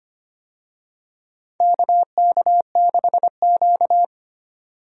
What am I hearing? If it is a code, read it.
Morse code, heard as a keyed tone, about 25 wpm, 700 Hz. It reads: KK6Q
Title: USA Call Signs - 4 Characters - 25wpm